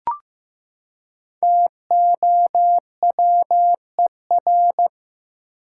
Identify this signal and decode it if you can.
Morse code, heard as a keyed tone, about 15 wpm, 700 Hz. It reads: TOWER
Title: Five Letter Words - 15wpm